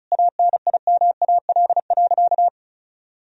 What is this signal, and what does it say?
Morse code, heard as a keyed tone, about 34 wpm, 700 Hz. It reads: ANIMAL.
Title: Sentences from Top 1000 Words - Review Part 1 - 35wpm